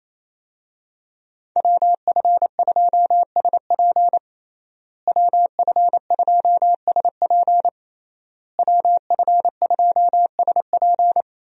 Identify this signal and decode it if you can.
Morse code, heard as a keyed tone, about 27 wpm, 700 Hz. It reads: WF2SP WF2SP WF2SP
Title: USA Call Signs - 5 Characters - Repeated 3x - 28wpm